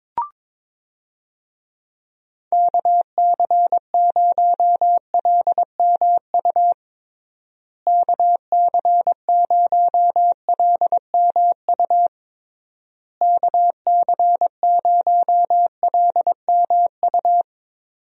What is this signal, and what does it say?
Morse code, heard as a keyed tone, about 22 wpm, 700 Hz. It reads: KC0LMU KC0LMU KC0LMU
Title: USA Call Signs - 6 Characters - Repeated 3x - 22wpm